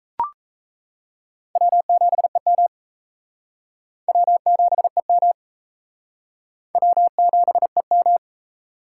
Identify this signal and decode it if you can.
Morse code, heard as a keyed tone, about 34 wpm, 700 Hz. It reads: W7EM W7EM W7EM
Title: Speed Racing - USA 1x2 Call Signs 28wpm